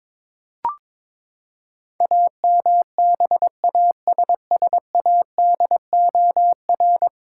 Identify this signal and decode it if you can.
Morse code, heard as a keyed tone, about 21 wpm, 700 Hz. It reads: AMBASSADOR
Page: morse-code-ninja-practice-22wpm